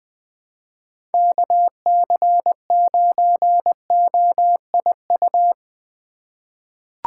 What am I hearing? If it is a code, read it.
Morse code, heard as a keyed tone, about 20 wpm, 700 Hz. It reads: KC9OIU
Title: USA Call Signs - 6 Characters - 3x Character Spacing - 20wpm